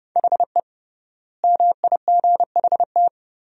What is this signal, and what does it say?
Morse code, heard as a keyed tone, about 30 wpm, 700 Hz. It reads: HE MIGHT
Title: 3x Word-Spacing - Sentences from Top 300 Words - 30wpm